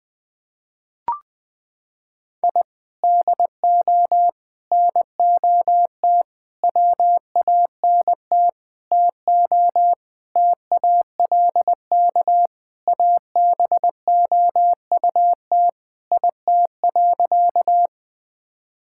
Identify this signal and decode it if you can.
Morse code, heard as a keyed tone, about 21 wpm, 700 Hz. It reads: I DO NOT WANT TO TALK ABOUT IT.
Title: Sentences with 5-Word Phrases - 20wpm